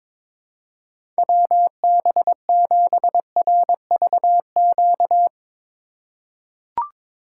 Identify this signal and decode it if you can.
Morse code, heard as a keyed tone, about 22 wpm, 700 Hz. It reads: WB7RVQ
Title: USA Call Signs - 4x Character Spacing - 22wpm